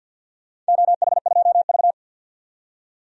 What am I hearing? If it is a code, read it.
Morse code, heard as a keyed tone, about 52 wpm, 700 Hz. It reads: OH2V
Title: DXCC-100 Call Signs - 4 Characters - 50wpm